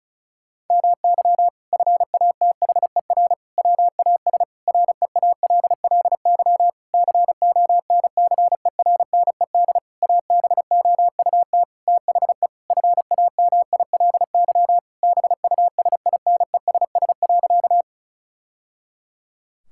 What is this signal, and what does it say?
Morse code, heard as a keyed tone, about 34 wpm, 700 Hz. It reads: MY FATHER WAS REALLY CONCERNED ABOUT THE FAMILY BUSINESS.
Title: Sentences from Top 1000 Words - Review Part 6 - 35wpm